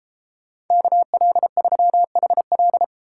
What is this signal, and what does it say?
Morse code, heard as a keyed tone, about 30 wpm, 700 Hz. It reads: KL3HL